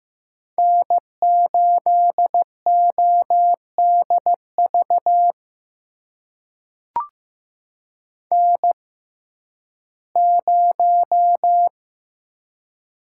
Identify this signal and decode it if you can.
Morse code, heard as a keyed tone, about 15 wpm, spaced as for 11 wpm, 700 Hz. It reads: N8ODV N0
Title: USA Call Signs - 5 Characters - 6x Character Spacing - 15wpm